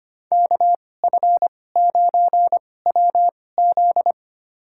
Morse code, heard as a keyed tone, about 25 wpm, spaced as for 18 wpm, 700 Hz. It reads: KF9WZ
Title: USA Call Signs - 5 Characters - 2x Character Spacing - 25wpm